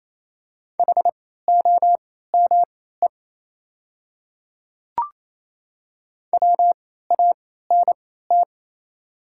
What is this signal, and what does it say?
Morse code, heard as a keyed tone, about 28 wpm, spaced as for 16 wpm, 690 Hz. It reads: HOME WANT
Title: 3x Character Spacing - Top 200 Words - 28wpm